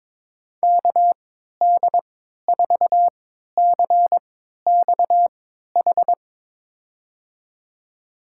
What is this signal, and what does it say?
Morse code, heard as a keyed tone, about 21 wpm, spaced as for 12 wpm, 700 Hz. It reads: KD4CXH
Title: USA Call Signs - 6 Characters - 3x Character Spacing - 22wpm